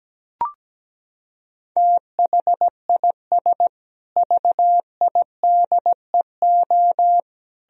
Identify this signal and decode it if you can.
Morse code, heard as a keyed tone, about 17 wpm, 700 Hz. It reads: THIS VIDEO
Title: Sentences from Top 1000 Words - Review Part 4 - 17wpm